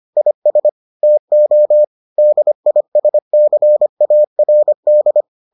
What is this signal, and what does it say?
Morse code, heard as a keyed tone, about 25 wpm, 600 Hz. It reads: IS TO DISCARD